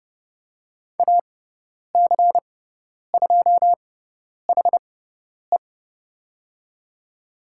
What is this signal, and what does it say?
Morse code, heard as a keyed tone, about 30 wpm, spaced as for 10 wpm, 700 Hz. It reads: AC2HE
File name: Lesson 16g - Letter c - Call Signs 30wpm